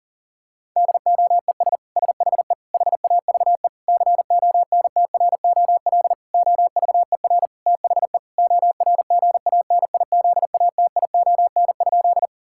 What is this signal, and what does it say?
Morse code, heard as a keyed tone, about 40 wpm, 700 Hz. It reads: DOES SHE HAVE CONTROL OVER THE ORGANIZATION?
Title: Sentences from Top 1000 Words - Review Part 5 - 40wpm